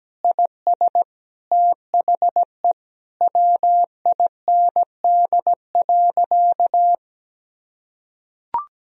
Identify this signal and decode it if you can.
Morse code, heard as a keyed tone, about 17 wpm, 700 Hz. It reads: IS THE WIND.
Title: Sentences from Top 1000 Words - Review Part 2 - 17wpm